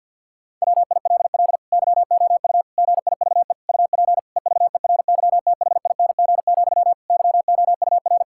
Morse code, heard as a keyed tone, about 50 wpm, 700 Hz. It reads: WILL YOU GIVE UP EVERYTHING, YOUR
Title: Sentences from Top 1000 Words - Review Part 5 - 50wpm